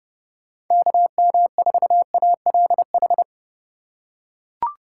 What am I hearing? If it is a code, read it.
Morse code, heard as a keyed tone, about 32 wpm, 700 Hz. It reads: KM4ALH
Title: USA Call Signs - 6 Characters - 2x Character Spacing - 30wpm